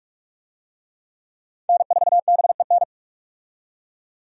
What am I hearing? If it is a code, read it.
Morse code, heard as a keyed tone, about 41 wpm, 700 Hz. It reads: N4BEN
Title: USA Call Signs - 5 Characters - 45wpm